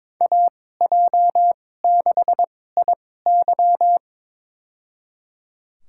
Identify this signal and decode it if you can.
Morse code, heard as a keyed tone, about 22 wpm, spaced as for 16 wpm, 700 Hz. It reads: AJ6IY
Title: USA Call Signs - 5 Characters - 2x Character Spacing - 22wpm